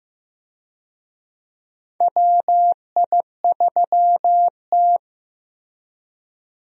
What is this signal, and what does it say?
Morse code, heard as a keyed tone, about 15 wpm, 700 Hz. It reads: WI3T
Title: USA Call Signs - 4 Characters - Repeated 3x - 15wpm